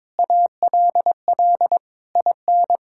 Morse code, heard as a keyed tone, about 21 wpm, 700 Hz. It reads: ALL IN
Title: Sentences from Top 1600 Words 22wpm